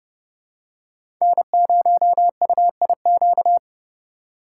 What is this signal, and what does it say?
Morse code, heard as a keyed tone, about 30 wpm, 700 Hz. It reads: N0UIQ
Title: USA Call Signs - 5 Characters - 2x Character Spacing - 30wpm